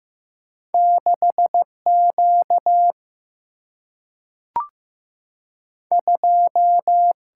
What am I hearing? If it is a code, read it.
Morse code, heard as a keyed tone, about 15 wpm, 700 Hz. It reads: 6 Q 2
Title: Single-Letter Single-Number 15wpm